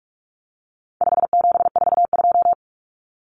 Morse code, heard as a keyed tone, about 44 wpm, 700 Hz. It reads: 5742
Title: Common Cut Numbers - 4-Digit 45wpm